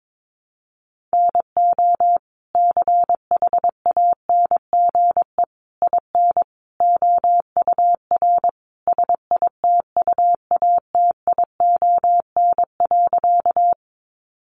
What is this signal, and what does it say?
Morse code, heard as a keyed tone, about 22 wpm, 700 Hz. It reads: NO CHANGE IN OUR SITUATION.
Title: Sentences from Top 1200 Words 22wpm